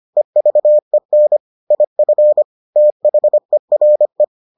Morse code, heard as a keyed tone, about 25 wpm, 600 Hz. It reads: EVEN IF THERE